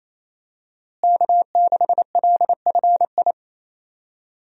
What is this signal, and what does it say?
Morse code, heard as a keyed tone, about 27 wpm, 700 Hz. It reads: K6LFI
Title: USA Call Signs - 5 Characters - Repeated 3x - 28wpm